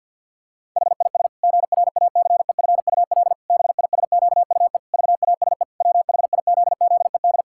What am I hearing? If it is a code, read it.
Morse code, heard as a keyed tone, about 52 wpm, 700 Hz. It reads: HIS GRACEFUL BISQUE VASE WHIZZED
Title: Pangram Challenge v2 - 50wpm